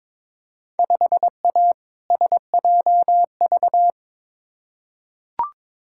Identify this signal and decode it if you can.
Morse code, heard as a keyed tone, about 22 wpm, 700 Hz. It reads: 5A SJV
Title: Field Day Exchanges - 22wpm